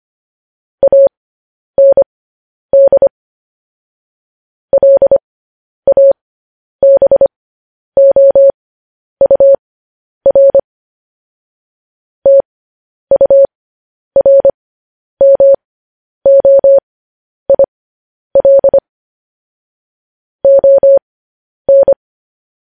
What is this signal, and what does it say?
Morse code, heard as a keyed tone, about 25 wpm, spaced as for 10 wpm, 550 Hz. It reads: AND LABOUR TURMOIL ON
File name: podcast / News headlines-1x-25x10